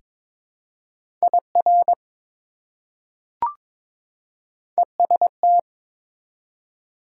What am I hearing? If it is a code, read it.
Morse code, heard as a keyed tone, about 22 wpm, 700 Hz. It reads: IR EST
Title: Prefixes and Suffixes 22wpm